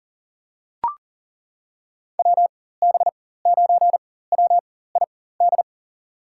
Morse code, heard as a keyed tone, about 40 wpm, spaced as for 19 wpm, 690 Hz. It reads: WB9WID
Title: USA Call Signs - 4x Character Spacing - 40wpm